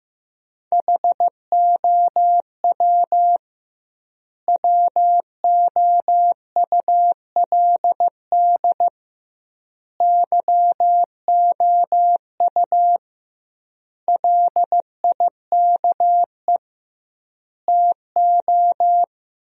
Morse code, heard as a keyed tone, about 15 wpm, 700 Hz. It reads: HOW WOULD YOU LIKE TO
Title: 2x Word-Spacing - Sentences from Top 100 Words - 15wpm